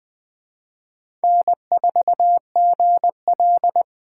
Morse code, heard as a keyed tone, about 21 wpm, 700 Hz. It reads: N4GL
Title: Speed Racing - USA 1x2 Call Signs 17wpm